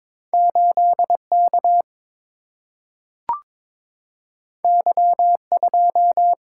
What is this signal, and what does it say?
Morse code, heard as a keyed tone, about 21 wpm, 700 Hz. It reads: 8 K Y 2
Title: Single-Letter Single-Number 22wpm